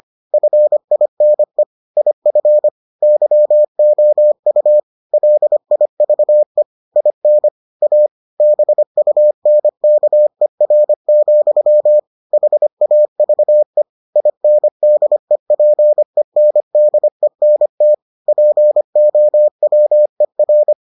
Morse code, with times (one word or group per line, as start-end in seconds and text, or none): 0.3-1.6 FINE
2.0-2.7 IF
3.0-4.8 YOU
5.1-6.6 LIVE
7.0-7.5 IN
7.8-8.1 A
8.4-12.0 BUNKER,
12.3-13.8 HAVE
14.2-17.9 INDEPENDENT
18.3-20.7 POWER